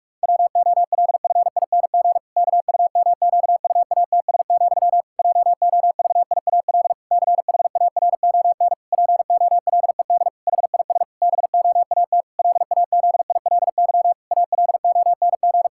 Morse code, heard as a keyed tone, about 41 wpm, 700 Hz. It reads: WOLFING KUMQUATS, JOVIAL CHARON POLED HIS BOAT LAZILY ALONG
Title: Pangram Challenge v2 - 45wpm